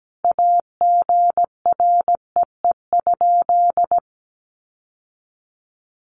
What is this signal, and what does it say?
Morse code, heard as a keyed tone, about 17 wpm, 700 Hz. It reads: AGREE?
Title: Sentences from Top 1100 Words 17wpm